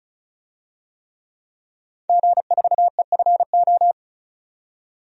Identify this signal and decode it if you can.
Morse code, heard as a keyed tone, about 34 wpm, 700 Hz. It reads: G4EFO
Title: DXCC-100 Call Signs - 5 Characters - 35wpm